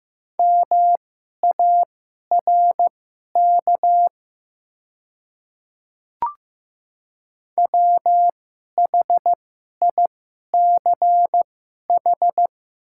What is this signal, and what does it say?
Morse code, heard as a keyed tone, about 15 wpm, spaced as for 11 wpm, 700 Hz. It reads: MARK WHICH
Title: 2x Character Spacing - Top 300 Words - 15wpm